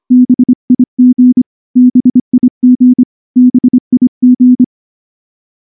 Morse code, heard as a keyed tone, about 24 wpm, 250 Hz. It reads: BIG BIG BIG
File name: morse-training-25WPM